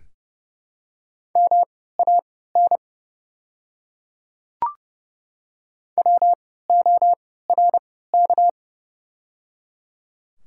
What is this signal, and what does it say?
Morse code, heard as a keyed tone, about 30 wpm, spaced as for 17 wpm, 690 Hz. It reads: MAN WORK
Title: 3x Character Spacing - Top 200 Words - 30wpm